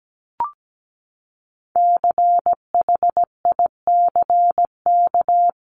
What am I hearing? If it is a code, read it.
Morse code, heard as a keyed tone, about 17 wpm, 700 Hz. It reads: CHICK
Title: CWT Exchange 17wpm